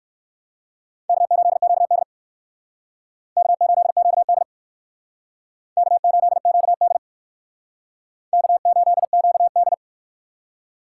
Morse code, heard as a keyed tone, about 56 wpm, 700 Hz. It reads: K8QD K8QD K8QD K8QD